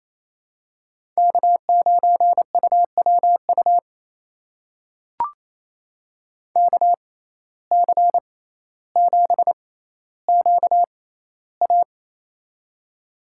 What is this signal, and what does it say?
Morse code, heard as a keyed tone, about 28 wpm, spaced as for 14 wpm, 700 Hz. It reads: K9UWU KC7QA
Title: USA Call Signs - 5 Characters - 6x Character Spacing - 28wpm